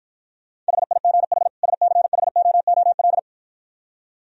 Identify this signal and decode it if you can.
Morse code, heard as a keyed tone, about 53 wpm, 700 Hz. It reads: HIGH SCHOOL